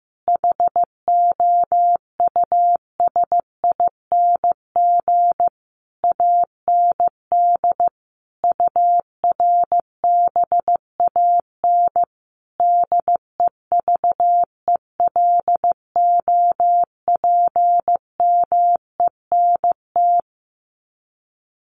Morse code, heard as a keyed tone, about 15 wpm, 700 Hz. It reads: HOUSING AND URBAN DEVELOPMENT
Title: Sets of 5 Words Encore2 15wpm